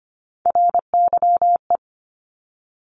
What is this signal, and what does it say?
Morse code, heard as a keyed tone, about 25 wpm, 700 Hz. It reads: RYE
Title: Multi-letter Phonemes Repeated 3x 25wpm